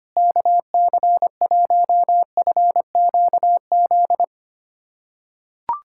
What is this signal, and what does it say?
Morse code, heard as a keyed tone, about 25 wpm, 700 Hz. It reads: KC1FQZ
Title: USA Callsign DE Callsign 25wpm